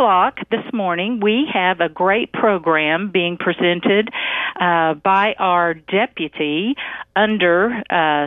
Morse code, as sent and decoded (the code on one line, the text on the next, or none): none